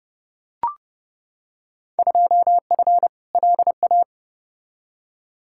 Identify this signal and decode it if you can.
Morse code, heard as a keyed tone, about 32 wpm, 700 Hz. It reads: 2F LA
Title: Field Day Exchanges - 30wpm